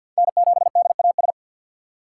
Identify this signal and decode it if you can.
Morse code, heard as a keyed tone, about 46 wpm, 700 Hz. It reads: N7DAS
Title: USA Call Signs - 5 Characters - 2x Character Spacing - 50wpm